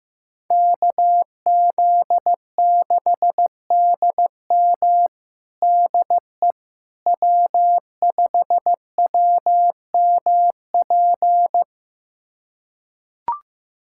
Morse code, heard as a keyed tone, about 15 wpm, 700 Hz. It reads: KZ6DM DE W5WMP
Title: USA Callsign DE Callsign - 5 Characters - 15wpm